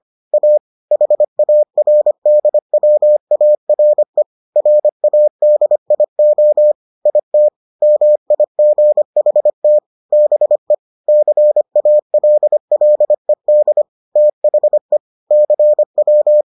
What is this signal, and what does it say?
Morse code, heard as a keyed tone, about 25 wpm, 600 Hz. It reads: A HARDWARE RADIO IT MIGHT BE CALLED THE CW